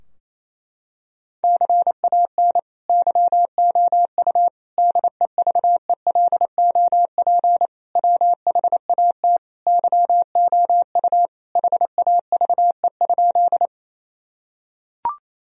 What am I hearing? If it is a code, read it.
Morse code, heard as a keyed tone, about 28 wpm, 700 Hz. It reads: CAN YOU DEVELOP WHAT YOU HAVE?
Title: Sentences from Top 1000 Words - Review Part 2 - 28wpm